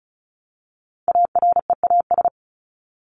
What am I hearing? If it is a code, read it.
Morse code, heard as a keyed tone, about 34 wpm, 700 Hz. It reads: AREAS
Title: ES:S Suffix - Top 100 Words with Base Words 35wpm